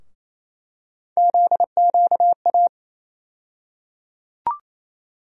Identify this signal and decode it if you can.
Morse code, heard as a keyed tone, about 28 wpm, 700 Hz. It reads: ZQA
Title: USA Call Sign Postfixes - 3 Letters 28wpm